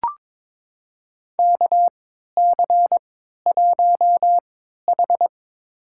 Morse code, heard as a keyed tone, about 22 wpm, spaced as for 12 wpm, 700 Hz. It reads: KC1H
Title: USA Call Signs - 4 Characters - 3x Character Spacing - 22wpm